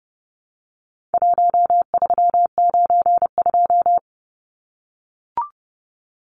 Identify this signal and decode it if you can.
Morse code, heard as a keyed tone, about 30 wpm, 700 Hz. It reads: 1392
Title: Numbers - Contest Style - 4-Digit 30wpm